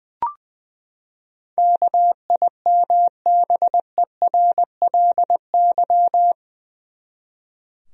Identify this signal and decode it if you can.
Morse code, heard as a keyed tone, about 20 wpm, 700 Hz. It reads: KIMBERLY
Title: USA Common Names - 20wpm